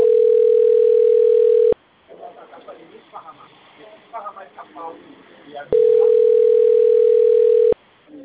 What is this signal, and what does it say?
Ringback tone (North America), 440 + 480 Hz, with speech.
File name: rg-1600-unknown- - - .84974